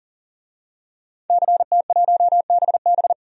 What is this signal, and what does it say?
Morse code, heard as a keyed tone, about 40 wpm, 700 Hz. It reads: CT1BB
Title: DXCC-100 Call Signs - 5 Characters - 40wpm